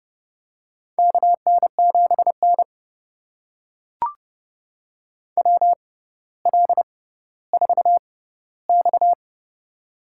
Morse code, heard as a keyed tone, about 30 wpm, spaced as for 15 wpm, 700 Hz. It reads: KN7N WL4X
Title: USA Call Signs - 4 Characters - 6x Character Spacing - 30wpm